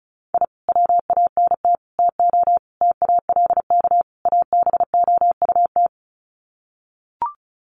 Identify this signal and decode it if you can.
Morse code, heard as a keyed tone, about 34 wpm, 700 Hz. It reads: I WANT TO TALK ABOUT
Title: Sets of 5 Words Encore 35wpm